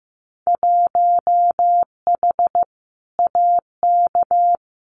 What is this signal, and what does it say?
Morse code, heard as a keyed tone, about 15 wpm, 700 Hz. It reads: 1H AK
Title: Winter Field Day Exchanges Encore 15wpm